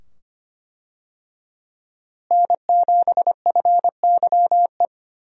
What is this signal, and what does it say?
Morse code, heard as a keyed tone, about 25 wpm, 700 Hz. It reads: N7FYE
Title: USA Call Signs - 5 Characters - Repeated 2x - 25wpm